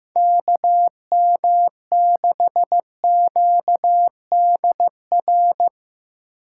Morse code, heard as a keyed tone, about 17 wpm, 700 Hz. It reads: KM6QDR
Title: Speed Racing - USA 2x3 Call Signs 15wpm